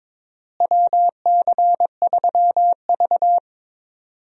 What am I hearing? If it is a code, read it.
Morse code, heard as a keyed tone, about 22 wpm, 700 Hz. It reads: WC3V
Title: USA Call Signs - 4 Characters - Repeated 3x - 22wpm